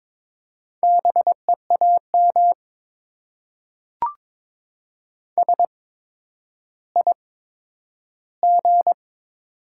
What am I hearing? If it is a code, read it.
Morse code, heard as a keyed tone, about 22 wpm, spaced as for 11 wpm, 700 Hz. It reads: BEAM SIG
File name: Top 90 QSO Elements - 8x Character Spacing - 22wpm